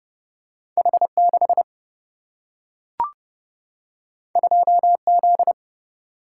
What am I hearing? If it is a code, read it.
Morse code, heard as a keyed tone, about 30 wpm, 700 Hz. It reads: H 6 2 Z
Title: Single-Letter Single-Number 30wpm